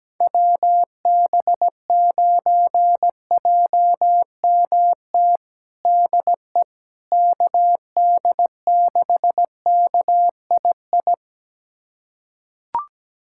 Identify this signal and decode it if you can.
Morse code, heard as a keyed tone, about 17 wpm, 700 Hz. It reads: WB9JMT DE KD6KII
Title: USA Callsign DE Callsign - 6 Characters - 17wpm